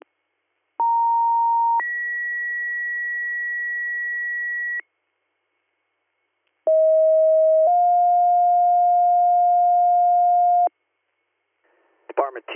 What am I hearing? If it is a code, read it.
Two-tone page, heard as a one-second tone, then a three-second tone, about 930 then 1950 Hz, about 640 then 710 Hz, with speech. It is heard from 0.8 to 10.7 s.